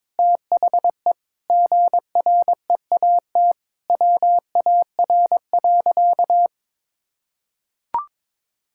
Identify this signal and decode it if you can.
Morse code, heard as a keyed tone, about 22 wpm, 700 Hz. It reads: THE GREAT WAR.
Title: Sentences from Top 1600 Words 22wpm